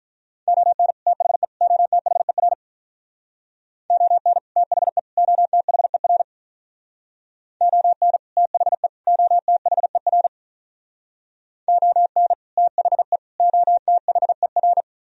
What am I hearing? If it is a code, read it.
Morse code, heard as a keyed tone, about 45 wpm, 700 Hz. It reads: ON THE OTHER ON THE OTHER ON THE OTHER ON THE OTHER